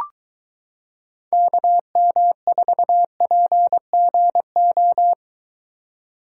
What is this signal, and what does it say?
Morse code, heard as a keyed tone, about 22 wpm, 700 Hz. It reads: KM4PGO